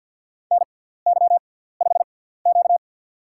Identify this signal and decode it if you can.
Morse code, heard as a keyed tone, about 51 wpm, spaced as for 17 wpm, 700 Hz. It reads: NY5Q
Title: USA Call Signs - 6x Character Spacing - 50wpm